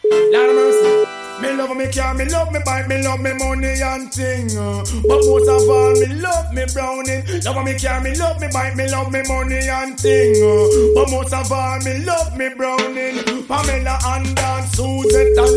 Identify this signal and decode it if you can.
Ringback tone (Europe, 425 Hz), with speech.